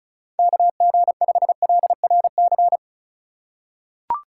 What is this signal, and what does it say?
Morse code, heard as a keyed tone, about 36 wpm, 700 Hz. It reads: KG5LRC
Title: USA Call Signs - 3x Character Spacing - 35wpm